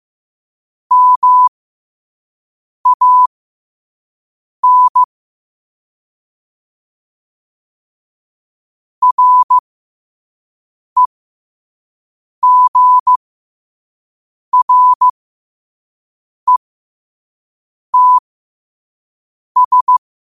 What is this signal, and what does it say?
Morse code, heard as a keyed tone, about 15 wpm, spaced as for 5 wpm, 1000 Hz. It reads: MAN REGRETS
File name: qotdfiles / 5wpm